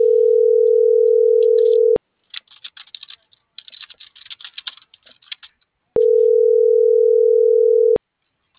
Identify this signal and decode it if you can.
Ringback tone (North America), 440 + 480 Hz.